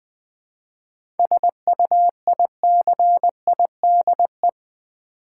Morse code, heard as a keyed tone, about 20 wpm, 700 Hz. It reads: SUICIDE
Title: 7 Letter Words - First 3 Letters Given - 20wpm